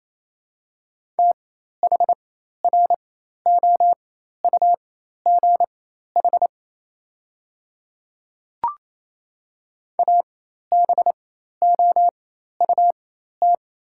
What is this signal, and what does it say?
Morse code, heard as a keyed tone, about 28 wpm, spaced as for 13 wpm, 690 Hz. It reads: THROUGH ABOUT
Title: 4x Character Spacing - Top 300 Words - 28wpm